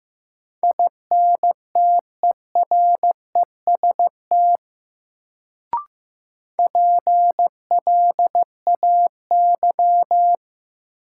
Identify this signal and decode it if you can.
Morse code, heard as a keyed tone, about 15 wpm, 700 Hz. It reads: INTEREST PLAY
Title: Sets of 5 Words Encore - New Word Drill - 15wpm